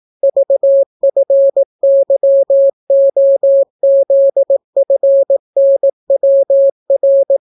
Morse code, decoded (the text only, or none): VFYOZFNWR